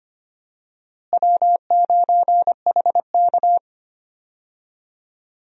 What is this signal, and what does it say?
Morse code, heard as a keyed tone, about 25 wpm, 700 Hz. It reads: W9HK
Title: USA Call Signs - 4 Characters - Repeated 2x - 25wpm